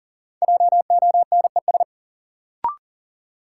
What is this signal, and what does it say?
Morse code, heard as a keyed tone, about 40 wpm, 700 Hz. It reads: JONES
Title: Sets of 5 Words - New Word Drill - 40wpm